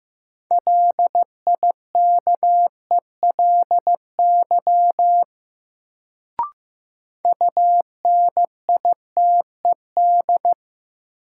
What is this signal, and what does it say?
Morse code, heard as a keyed tone, about 15 wpm, 700 Hz. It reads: LIKELY UNITED
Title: Sentences from Top 600 Words - New Word Drill - 15wpm